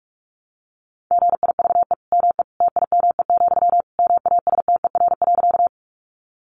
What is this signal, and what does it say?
Morse code, heard as a keyed tone, about 44 wpm, 700 Hz. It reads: GIVE ME TIME, MASTER.
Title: Sentences from Top 1300 Words 45wpm